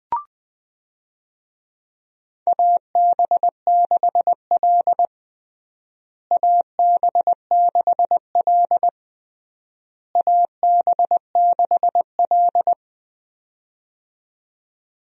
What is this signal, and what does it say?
Morse code, heard as a keyed tone, about 20 wpm, 690 Hz. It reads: AB6L AB6L AB6L
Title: USA Call Signs - 4 Characters - Repeated 3x - 20wpm